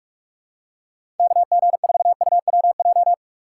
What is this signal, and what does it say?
Morse code, heard as a keyed tone, about 46 wpm, 700 Hz. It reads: KG4UWJ
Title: USA Call Signs - 6 Characters - 3x Character Spacing - 45wpm